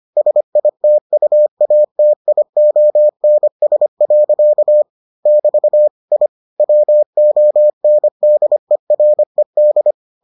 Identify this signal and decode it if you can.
Morse code, heard as a keyed tone, about 25 wpm, 600 Hz. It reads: SITUATIONS. <BT> I WONDERED